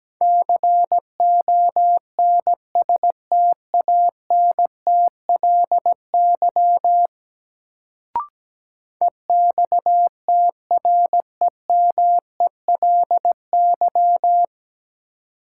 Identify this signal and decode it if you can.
Morse code, heard as a keyed tone, about 17 wpm, 700 Hz. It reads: CONSTANTLY EXTREMELY
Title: Sentences from Top 1000 Words - Review Part 7 - New Word Drill - 17wpm